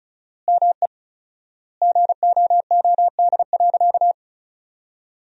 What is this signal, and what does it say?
Morse code, heard as a keyed tone, about 34 wpm, 700 Hz. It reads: ME GOOD.
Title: 4x Word-Spacing - Sentences from Top 200 Words - 35wpm